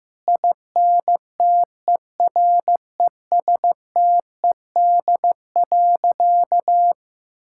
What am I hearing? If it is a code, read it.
Morse code, heard as a keyed tone, about 16 wpm, 700 Hz. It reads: INTERESTED.
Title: Sentences from Top 1000 Words - Review Part 6 - 15wpm